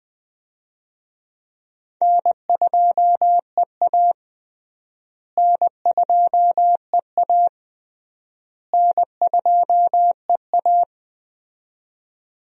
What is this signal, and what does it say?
Morse code, heard as a keyed tone, about 20 wpm, 700 Hz. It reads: N2EA N2EA N2EA
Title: USA Call Signs - 4 Characters - Repeated 3x - 20wpm